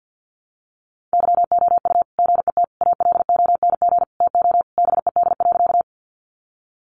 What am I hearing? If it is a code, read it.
Morse code, heard as a keyed tone, about 48 wpm, 700 Hz. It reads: YOU GET ALONG TO BED.
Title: Sentences from Top 700 Words 50wpm